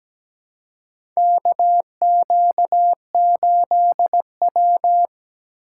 Morse code, heard as a keyed tone, about 17 wpm, 700 Hz. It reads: KQ8W
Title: USA Call Signs - 4 Characters - 6x Character Spacing - 17wpm